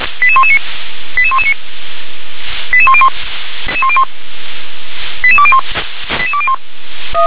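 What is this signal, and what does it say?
Five-tone selective call (ZVEI), heard as quick runs of five tones, tones of about 70 ms. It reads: 88119 88119 88101 88201 88201
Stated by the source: ZVEI five-tone call received by SDR.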